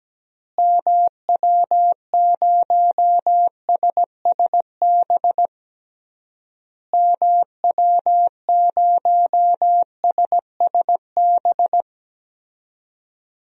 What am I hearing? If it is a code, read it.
Morse code, heard as a keyed tone, about 17 wpm, 700 Hz. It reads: MW0SSB MW0SSB
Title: DXCC-100 Call Signs - 6 Characters - Repeated 3x - 17wpm